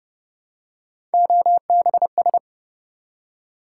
Morse code, heard as a keyed tone, about 31 wpm, 700 Hz. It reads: OBS